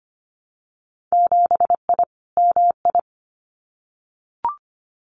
Morse code, heard as a keyed tone, about 25 wpm, 700 Hz. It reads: 7I MI
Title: Winter Field Day Exchanges Encore 25wpm